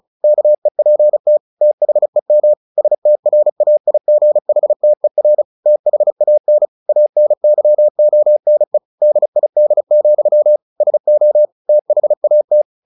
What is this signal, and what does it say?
Morse code, heard as a keyed tone, about 34 wpm, 600 Hz. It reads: KEPT THEM STRAIGHTER THAN ANYONE DID, SO THAT
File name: the old man and the sea / the old man and the sea 1 35wpm